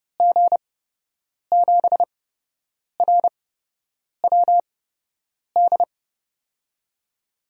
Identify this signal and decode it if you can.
Morse code, heard as a keyed tone, about 30 wpm, spaced as for 8 wpm, 700 Hz. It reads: G7RWD
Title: USA Call Signs - 6 Characters - 8x Character Spacing - 30wpm